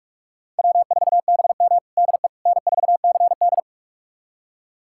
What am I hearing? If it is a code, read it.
Morse code, heard as a keyed tone, about 46 wpm, 700 Hz. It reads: W4BM DE N4CD
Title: USA Callsign DE Callsign - 4 Characters - 45wpm